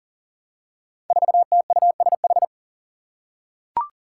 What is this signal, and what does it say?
Morse code, heard as a keyed tone, about 37 wpm, 700 Hz. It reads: 4TUSH
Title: Five Challenging Letter-Number Combinations 40wpm